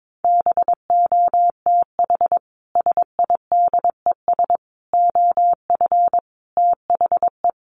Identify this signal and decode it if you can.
Morse code, heard as a keyed tone, about 22 wpm, 700 Hz. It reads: BOTH SIDES OF THE
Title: Sets of 5 Words Encore2 22wpm